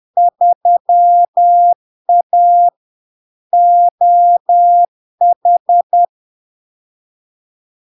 Morse code, heard as a keyed tone, about 10 wpm, 700 Hz. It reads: 3A OH
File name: LARK-FD-2022-Log-in-CW-10wpm-700-Hz